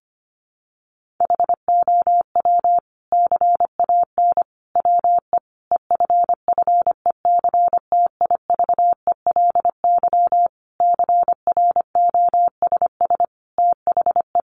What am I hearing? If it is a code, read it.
Morse code, heard as a keyed tone, about 25 wpm, 700 Hz. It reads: HOW CAN WE EFFECTIVELY CROSS THE
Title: Sentences from Top 1400 Words 25wpm